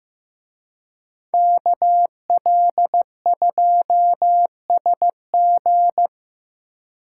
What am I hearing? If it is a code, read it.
Morse code, heard as a keyed tone, about 18 wpm, 700 Hz. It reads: KL2SG
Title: Speed Racing - USA Call Signs 15wpm